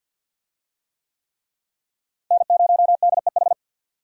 Morse code, heard as a keyed tone, about 49 wpm, 700 Hz. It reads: N0DEH
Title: USA Call Signs - 5 Characters - Repeated 3x - 50wpm